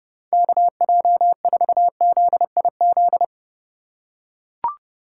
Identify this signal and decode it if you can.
Morse code, heard as a keyed tone, about 28 wpm, 700 Hz. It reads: KJ4ZIZ